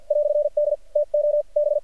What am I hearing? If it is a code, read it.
Morse code, heard as a keyed tone, about 50 wpm, spaced as for 42 wpm, 590 Hz. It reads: "D TO B
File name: archived / CRSnet